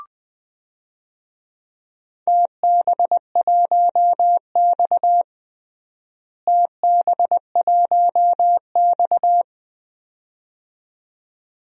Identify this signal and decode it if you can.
Morse code, heard as a keyed tone, about 20 wpm, 700 Hz. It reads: TB1X TB1X
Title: DXCC-100 Call Signs - 4 Characters - Repeated 2x - 20wpm